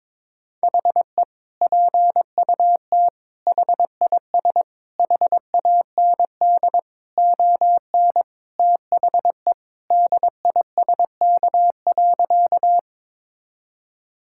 Morse code, heard as a keyed tone, about 22 wpm, 700 Hz. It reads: HE PUT HIS HAND ON THE DISK.
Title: Sentences from Top 1000 Words - Review Part 6 - 22wpm